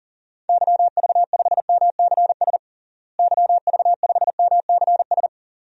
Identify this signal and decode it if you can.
Morse code, heard as a keyed tone, about 37 wpm, 700 Hz. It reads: YV5MCS YV5MCS
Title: DXCC-100 Call Signs - 6 Characters - Repeated 2x - 40wpm